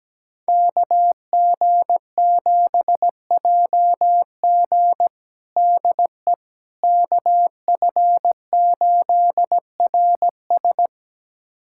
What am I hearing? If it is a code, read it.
Morse code, heard as a keyed tone, about 17 wpm, 700 Hz. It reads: KG7JG DE KF8RS
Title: USA Callsign DE Callsign - 5 Characters - 17wpm